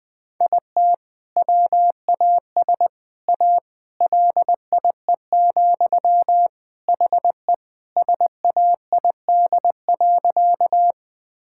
Morse code, heard as a keyed tone, about 20 wpm, 700 Hz. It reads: IT WAS A LIE, HE SAID.
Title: Sentences from Top 1600 Words 20wpm